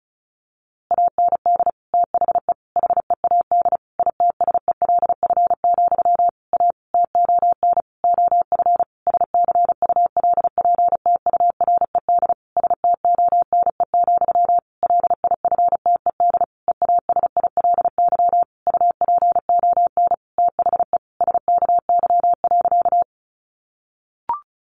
Morse code, heard as a keyed tone, about 34 wpm, 700 Hz. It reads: AND THE HEAD ITSELF, A TON OF SCULPTURED STONE, LIFTED EASILY UPON THE SKY.
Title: A Sound Of Thunder 35wpm